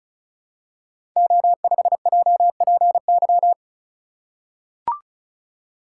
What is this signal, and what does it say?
Morse code, heard as a keyed tone, about 36 wpm, 700 Hz. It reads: O5JPY
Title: Code Groups - 5 Letter:Numbers - 35wpm